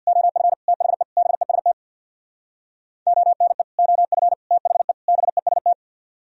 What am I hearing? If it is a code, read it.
Morse code, heard as a keyed tone, about 65 wpm, 700 Hz. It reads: OF THE BEST ONE OF THE BEST